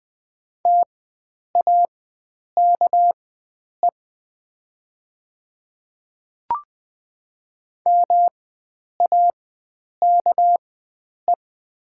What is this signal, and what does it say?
Morse code, heard as a keyed tone, about 20 wpm, spaced as for 9 wpm, 700 Hz. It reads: TAKE MAKE
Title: 4x Character Spacing - Top 300 Words - 20wpm